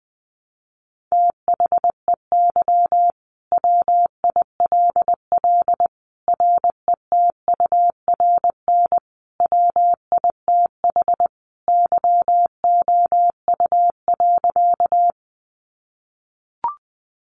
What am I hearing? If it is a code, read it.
Morse code, heard as a keyed tone, about 20 wpm, 700 Hz. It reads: THEY WILL RETURN WITH YOU.
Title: Sentences from Top 800 Words 20wpm